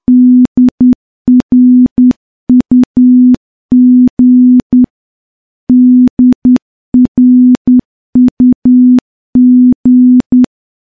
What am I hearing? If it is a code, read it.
Morse code, heard as a keyed tone, about 10 wpm, 250 Hz. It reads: DRUG DRUG
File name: morse-training-10WPM